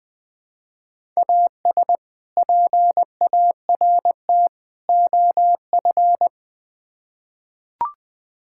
Morse code, heard as a keyed tone, about 20 wpm, 700 Hz. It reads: AS PART OF
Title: Build A Phrase - 3-Word Phrases - 20wpm